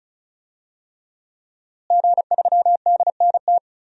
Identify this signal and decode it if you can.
Morse code, heard as a keyed tone, about 36 wpm, 700 Hz. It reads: G3DNT
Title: DXCC-100 Call Signs - 5 Characters - Repeated 3x - 35wpm